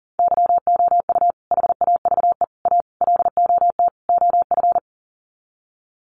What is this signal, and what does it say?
Morse code, heard as a keyed tone, about 40 wpm, 700 Hz. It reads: YOU HAVE A LOT OF
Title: Sets of 5 Words Encore 40wpm